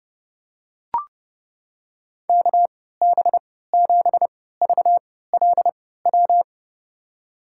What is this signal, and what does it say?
Morse code, heard as a keyed tone, about 30 wpm, spaced as for 17 wpm, 700 Hz. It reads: KB7VLW
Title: USA Call Signs - 6 Characters - 3x Character Spacing - 30wpm